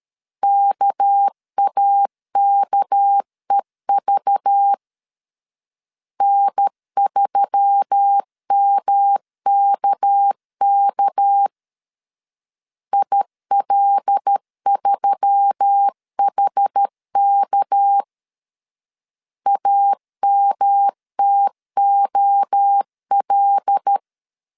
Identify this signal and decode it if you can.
Morse code, heard as a keyed tone, about 13 wpm, 800 Hz. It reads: KAKEV N3MKK IL3HK AMTOL
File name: LEZIONI / 6 PKX0 / 6 PK 12wpm 2006